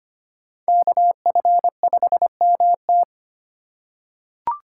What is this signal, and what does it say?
Morse code, heard as a keyed tone, about 25 wpm, 700 Hz. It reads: KF5MT
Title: USA Call Signs - 5 Characters - 4x Character Spacing - 25wpm